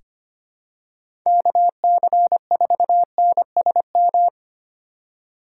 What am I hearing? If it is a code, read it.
Morse code, heard as a keyed tone, about 25 wpm, 700 Hz. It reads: KC4NSM
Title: USA Call Signs - 6 Characters - 2x Character Spacing - 25wpm